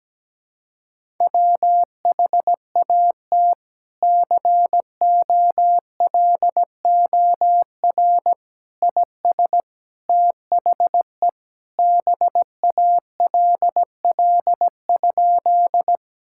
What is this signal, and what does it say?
Morse code, heard as a keyed tone, about 17 wpm, 700 Hz. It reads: WHAT COLOR IS THE BALL?
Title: Sentences from Top 1000 Words - Review Part 6 - 17wpm